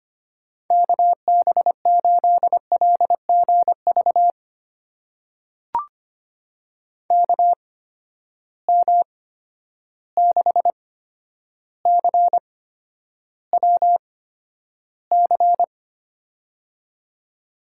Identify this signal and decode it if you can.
Morse code, heard as a keyed tone, about 25 wpm, spaced as for 11 wpm, 700 Hz. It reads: KB8LGV KM6CWC
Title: USA Call Signs - 6 Characters - 8x Character Spacing - 25wpm